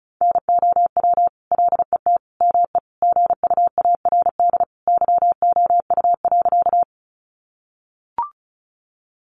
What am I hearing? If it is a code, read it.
Morse code, heard as a keyed tone, about 34 wpm, 700 Hz. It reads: NOW LET ME GUARD YOU.
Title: Sentences from Top 700 Words 35wpm